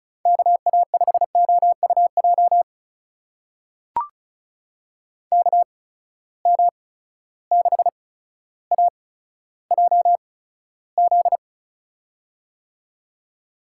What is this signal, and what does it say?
Morse code, heard as a keyed tone, about 34 wpm, spaced as for 15 wpm, 700 Hz. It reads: KA5OUJ KM6AJZ
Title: USA Call Signs - 6 Characters - 8x Character Spacing - 35wpm